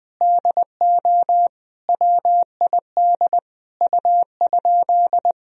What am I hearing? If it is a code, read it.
Morse code, heard as a keyed tone, about 21 wpm, 700 Hz. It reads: DO WID U?
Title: Sentences from Top 100 Words and Abbreviations 20wpm